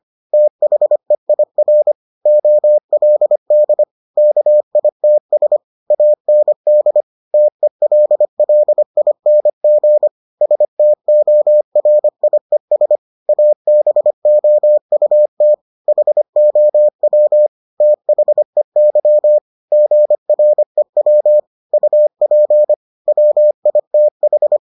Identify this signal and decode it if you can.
Morse code, heard as a keyed tone, about 25 wpm, 600 Hz. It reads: THEIR OLD KITS AND TELLING STORIES ABOUT HOW THEY GREW UP WITH